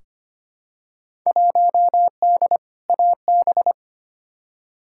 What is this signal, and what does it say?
Morse code, heard as a keyed tone, about 25 wpm, 700 Hz. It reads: 1D AB
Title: Field Day Exchanges Encore - 25wpm